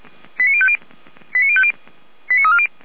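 Five-tone selective call (ZVEI), heard as quick runs of five tones, tones of about 70 ms. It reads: 89050 89050 89340
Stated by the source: ZVEI five-tone selective call received with an SDR, logged by the Feuerwehr Orlamünde fire service.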